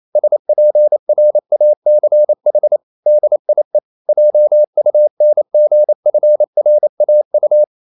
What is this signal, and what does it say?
Morse code, heard as a keyed tone, about 28 wpm, 600 Hz. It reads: SPRACH DIE JUNGFRAU
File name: marienkind9 28wpm